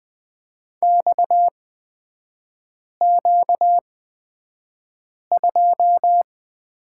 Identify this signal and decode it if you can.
Morse code, heard as a keyed tone, about 20 wpm, spaced as for 5 wpm, 690 Hz. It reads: XQ2
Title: 236 Lesson 38e - Letter q - Three Characters 20wpm